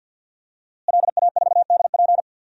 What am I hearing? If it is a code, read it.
Morse code, heard as a keyed tone, about 51 wpm, 700 Hz. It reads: LA4DP
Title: DXCC-100 Call Signs - 5 Characters - Repeated 2x - 50wpm